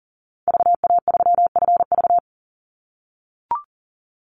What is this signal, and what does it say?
Morse code, heard as a keyed tone, about 40 wpm, 700 Hz. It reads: VA3FV
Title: Call Signs - 40wpm